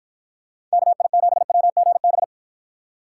Morse code, heard as a keyed tone, about 53 wpm, 700 Hz. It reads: KI7WGB